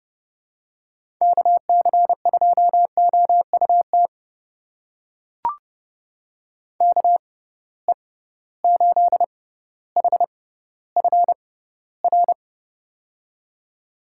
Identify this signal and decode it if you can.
Morse code, heard as a keyed tone, about 30 wpm, spaced as for 15 wpm, 700 Hz. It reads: KC2OUT KE8HFR
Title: USA Call Signs - 6 Characters - 6x Character Spacing - 30wpm